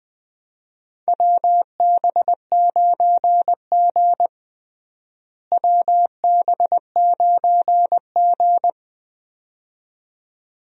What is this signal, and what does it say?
Morse code, heard as a keyed tone, about 20 wpm, 700 Hz. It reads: WB9G WB9G
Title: USA Call Signs - 4 Characters - Repeated 2x - 20wpm